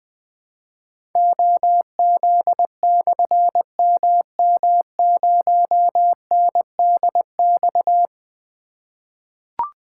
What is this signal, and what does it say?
Morse code, heard as a keyed tone, about 20 wpm, 700 Hz. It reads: OZ/MM0NDX
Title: Difficult Call Signs 20wpm